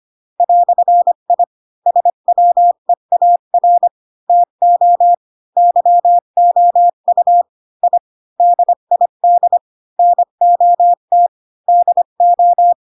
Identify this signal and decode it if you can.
Morse code, heard as a keyed tone, about 25 wpm, 700 Hz. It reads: "I SWEAR TO YOU I DID NOT DO